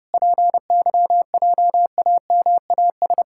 Morse code, heard as a keyed tone, about 30 wpm, 700 Hz. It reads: PYJAMAS
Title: Pangram Challenge v2 - 30wpm